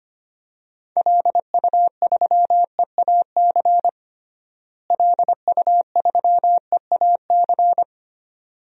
Morse code, heard as a keyed tone, about 25 wpm, 700 Hz. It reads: LU3EAC LU3EAC
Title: DXCC-100 Call Signs - 6 Characters - Repeated 2x - 25wpm